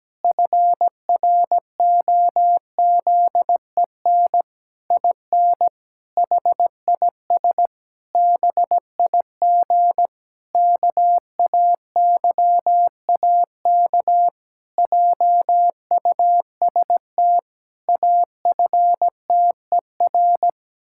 Morse code, heard as a keyed tone, about 17 wpm, 700 Hz. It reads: FROZEN IN HIS BIG KAYAK JUST AFTER
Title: Pangram Challenge v2 - 17wpm